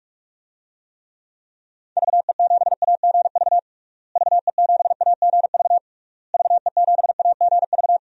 Morse code, heard as a keyed tone, about 46 wpm, 700 Hz. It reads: VE7AGV VE7AGV VE7AGV
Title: DXCC-100 Call Signs - 6 Characters - Repeated 3x - 45wpm